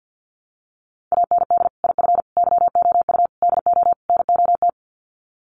Morse code, heard as a keyed tone, about 50 wpm, 700 Hz. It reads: AND IF YOU DO NOT
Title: Sets of 5 Words 50wpm